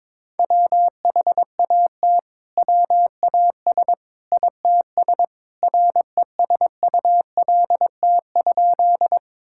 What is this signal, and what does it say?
Morse code, heard as a keyed tone, about 22 wpm, 700 Hz. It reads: WHAT WAS ITS RESULT?
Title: Sentences from Top 1000 Words - Review Part 5 - 22wpm